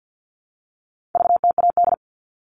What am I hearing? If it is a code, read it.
Morse code, heard as a keyed tone, about 48 wpm, 700 Hz. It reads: VTAD
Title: All Cut Numbers - 4-Digit 50wpm